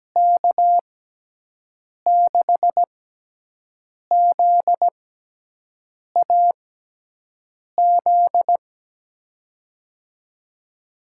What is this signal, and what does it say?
Morse code, heard as a keyed tone, about 17 wpm, spaced as for 6 wpm, 700 Hz. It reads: K6ZAZ
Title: USA Call Signs - 6 Characters - 6x Character Spacing - 17wpm